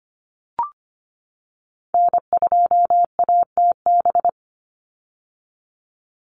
Morse code, heard as a keyed tone, about 25 wpm, 700 Hz. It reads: N2ATB
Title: Call Signs - 25wpm